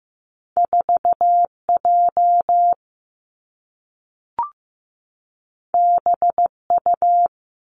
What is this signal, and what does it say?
Morse code, heard as a keyed tone, about 15 wpm, 700 Hz. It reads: 4 J B U
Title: Two Challenging Letter-Number Combinations 15wpm